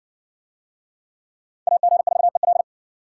Morse code, heard as a keyed tone, about 58 wpm, 700 Hz. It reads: AG4EL